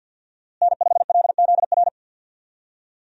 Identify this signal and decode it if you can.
Morse code, heard as a keyed tone, about 51 wpm, 700 Hz. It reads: N5LZR
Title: USA Call Signs - 5 Characters - Repeated 3x - 50wpm